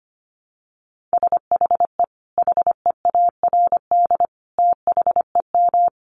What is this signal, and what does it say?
Morse code, heard as a keyed tone, about 25 wpm, 700 Hz. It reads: SHE HEARD THEM
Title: Sentences from Top 1300 Words 25wpm